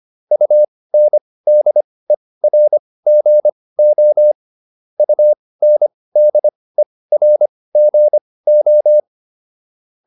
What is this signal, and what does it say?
Morse code, heard as a keyed tone, about 25 wpm, spaced as for 18 wpm, 600 Hz. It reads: UNDERGO UNDERGO